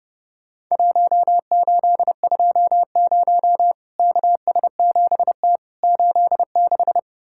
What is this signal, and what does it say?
Morse code, heard as a keyed tone, about 30 wpm, 700 Hz. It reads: 1820 KS7T 86
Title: Worked All Europe DX Contest - QTC Exchange - 30wpm